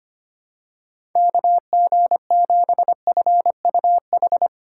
Morse code, heard as a keyed tone, about 25 wpm, 700 Hz. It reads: KG7FUH
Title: USA Call Signs - 2x Character Spacing - 25wpm